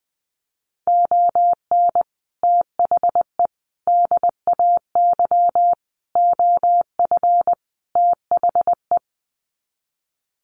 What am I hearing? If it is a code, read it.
Morse code, heard as a keyed tone, about 20 wpm, 700 Hz. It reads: ON THE DAY OF THE
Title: Sets of 5 Words Encore 20wpm